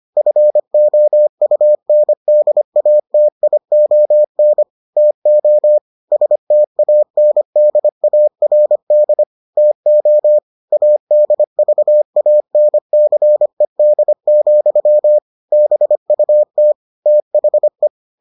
Morse code, heard as a keyed tone, about 25 wpm, 600 Hz. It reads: FOUNDATION TO STANDARD TO ADVANCED, BUT THE